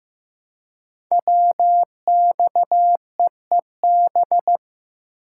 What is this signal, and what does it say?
Morse code, heard as a keyed tone, about 15 wpm, 700 Hz. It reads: WXEEB
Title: Code Groups - 5 Letters - 15wpm